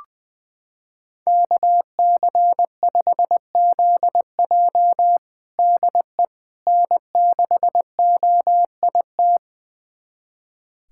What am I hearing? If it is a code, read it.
Morse code, heard as a keyed tone, about 21 wpm, 700 Hz. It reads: KC5ZJ DE N6OIT
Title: USA Callsign DE Callsign - 5 Characters - 20wpm